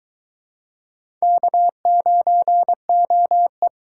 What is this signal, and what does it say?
Morse code, heard as a keyed tone, about 22 wpm, 700 Hz. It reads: K9OE